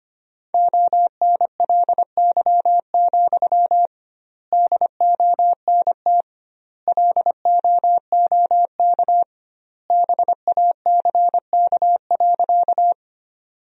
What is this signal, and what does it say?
Morse code, heard as a keyed tone, about 25 wpm, 700 Hz. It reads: ONLY, DONT LOOK BACK.
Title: 2x Word-Spacing - Sentences from Top 300 Words - 25wpm